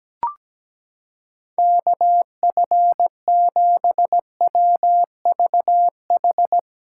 Morse code, heard as a keyed tone, about 17 wpm, 700 Hz. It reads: KF7WVH
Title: USA Callsign DE Callsign - 6 Characters - 17wpm